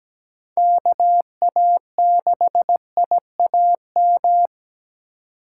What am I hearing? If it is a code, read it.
Morse code, heard as a keyed tone, about 17 wpm, 700 Hz. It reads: KA6IAM
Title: USA Call Signs - 6 Characters - 3x Character Spacing - 17wpm